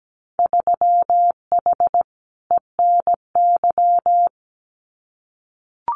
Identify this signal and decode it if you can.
Morse code, heard as a keyed tone, about 17 wpm, 700 Hz. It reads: 3H ENY
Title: Winter Field Day Exchanges Encore 17wpm